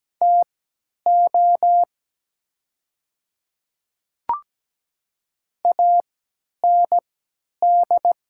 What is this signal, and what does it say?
Morse code, heard as a keyed tone, about 17 wpm, spaced as for 10 wpm, 700 Hz. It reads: TO AND
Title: 3x Character Spacing - Top 200 Words - 17wpm